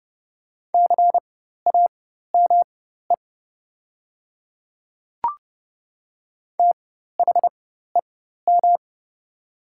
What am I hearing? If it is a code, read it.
Morse code, heard as a keyed tone, about 30 wpm, spaced as for 14 wpm, 700 Hz. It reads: CAME THEM
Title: 4x Character Spacing - Top 200 Words - 30wpm